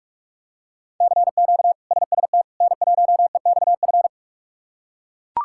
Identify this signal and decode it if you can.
Morse code, heard as a keyed tone, about 43 wpm, 700 Hz. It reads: CQ SST N1EXF
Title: K1USN SST QSO Practice - 45wpm